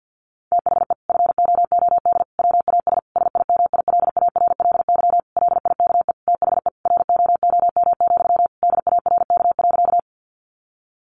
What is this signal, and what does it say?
Morse code, heard as a keyed tone, about 50 wpm, 700 Hz. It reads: THE FOOD WAS SIMILARLY LIKE THE ROOM, DARK.
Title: Sentences from Top 1400 Words 50wpm